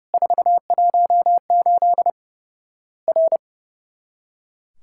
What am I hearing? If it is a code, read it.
Morse code, heard as a keyed tone, about 30 wpm, 700 Hz. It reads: 418 R
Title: Worked All Europe DX Contest - QTC Exchange - 30wpm